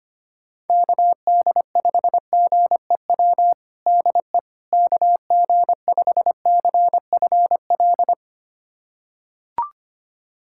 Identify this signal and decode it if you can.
Morse code, heard as a keyed tone, about 25 wpm, 700 Hz. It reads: KD5GEW DE KG5CFL
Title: USA Callsign DE Callsign - 6 Characters - 25wpm